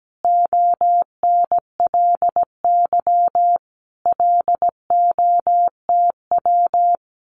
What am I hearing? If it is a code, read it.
Morse code, heard as a keyed tone, about 17 wpm, 700 Hz. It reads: ONLY LOTW
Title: Common QSO Phrases Encore 17wpm